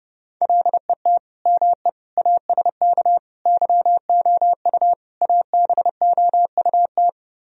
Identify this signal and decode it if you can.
Morse code, heard as a keyed tone, about 33 wpm, 700 Hz. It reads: LET ME ASK YOU ABOUT